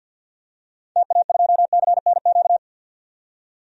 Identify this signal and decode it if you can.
Morse code, heard as a keyed tone, about 49 wpm, 700 Hz. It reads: TA2CNQ
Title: DXCC-100 Call Signs - 6 Characters - 50wpm